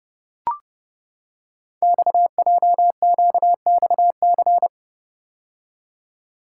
Morse code, heard as a keyed tone, about 28 wpm, 700 Hz. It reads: XJQXC
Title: Code Groups - 5 Letters - 30wpm